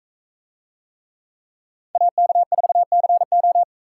Morse code, heard as a keyed tone, about 41 wpm, 700 Hz. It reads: AK4CO